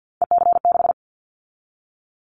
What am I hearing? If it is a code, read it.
Morse code, heard as a keyed tone, about 50 wpm, 700 Hz. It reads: EC6
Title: Callsign Prefixes for Top 50 DXCC - 50wpm